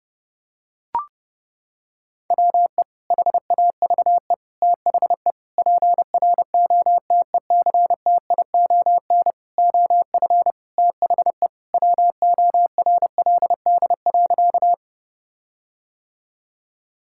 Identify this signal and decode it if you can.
Morse code, heard as a keyed tone, about 30 wpm, 700 Hz. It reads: WE HAVE THE PROTECTION OF THE WORLD.
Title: Sentences from Top 1600 Words 30wpm